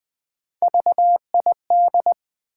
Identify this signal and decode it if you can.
Morse code, heard as a keyed tone, about 20 wpm, 700 Hz. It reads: VID
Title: Build A Word - 1 Letter At A Time - 5-Letter Words - 20wpm